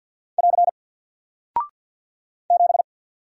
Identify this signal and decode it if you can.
Morse code, heard as a keyed tone, about 48 wpm, 700 Hz. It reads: <AR> 7
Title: Single Character (Letter:Number:Punctuation:Prosign) Alt 50wpm